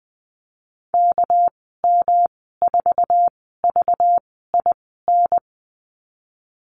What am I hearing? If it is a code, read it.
Morse code, heard as a keyed tone, about 20 wpm, spaced as for 14 wpm, 700 Hz. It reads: KM4VIN
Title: 200 Lesson 32g - Letter k - Call Signs 20wpm